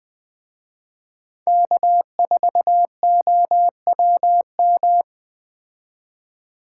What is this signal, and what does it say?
Morse code, heard as a keyed tone, about 20 wpm, 700 Hz. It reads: K4OWM